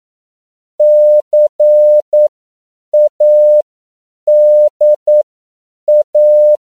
Morse code, heard as a keyed tone, about 9 wpm, spaced as for 7 wpm, 600 Hz. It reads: CADA